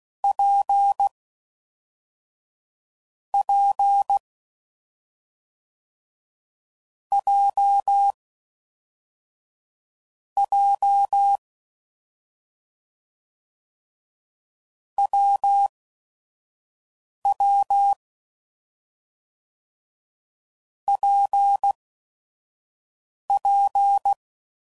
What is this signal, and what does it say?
Morse code, heard as a keyed tone, about 16 wpm, spaced as for 5 wpm, 800 Hz. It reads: P P J J WW PP
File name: Combinations / Difficult Combos / AWJ1